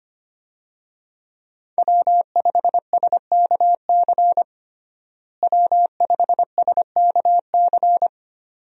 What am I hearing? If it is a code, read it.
Morse code, heard as a keyed tone, about 25 wpm, 700 Hz. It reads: W5SKC W5SKC
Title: USA Call Signs - 5 Characters - Repeated 2x - 25wpm